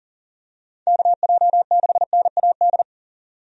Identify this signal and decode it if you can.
Morse code, heard as a keyed tone, about 40 wpm, 700 Hz. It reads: KJ6NAD
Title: USA Call Signs - 6 Characters - 40wpm